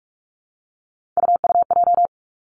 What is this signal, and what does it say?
Morse code, heard as a keyed tone, about 46 wpm, 700 Hz. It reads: UUJ
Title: Three Challenging Letter-Number Combinations 45wpm